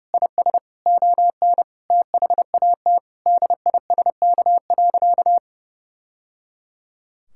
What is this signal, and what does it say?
Morse code, heard as a keyed tone, about 30 wpm, 700 Hz. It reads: IS ON THAT DISK.
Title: Sentences from Top 1000 Words - Review Part 6 - 30wpm